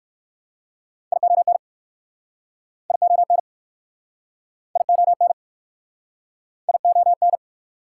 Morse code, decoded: ION ION ION ION

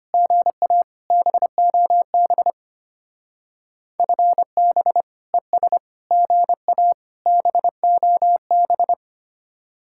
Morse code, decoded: GA BOB FB ES GA BOB

